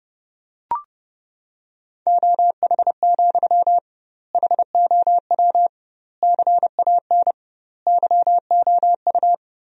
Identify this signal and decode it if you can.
Morse code, heard as a keyed tone, about 32 wpm, 700 Hz. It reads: OH, HOW CAN YOU
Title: 2x Word-Spacing - Sentences from Top 500 Words - 30wpm